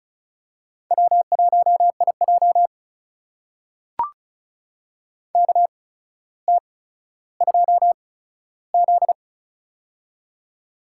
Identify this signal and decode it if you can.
Morse code, heard as a keyed tone, about 36 wpm, spaced as for 15 wpm, 700 Hz. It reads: W1IJ KT2Z
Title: USA Call Signs - 4 Characters - 8x Character Spacing - 35wpm